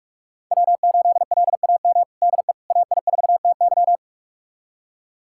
Morse code, heard as a keyed tone, about 46 wpm, 700 Hz. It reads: W8LAM DE AI4TY
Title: USA Callsign DE Callsign - 5 Characters - 45wpm